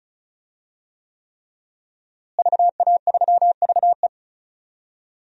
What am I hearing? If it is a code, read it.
Morse code, heard as a keyed tone, about 36 wpm, 700 Hz. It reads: VA3VE
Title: DXCC-100 Call Signs - 5 Characters - Repeated 3x - 35wpm